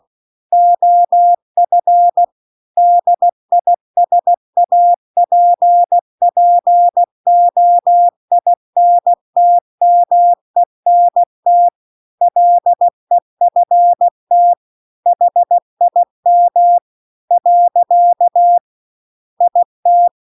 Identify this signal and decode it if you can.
Morse code, heard as a keyed tone, about 16 wpm, 700 Hz. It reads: OF DISAPPOINTMENT LEFT HIM . IT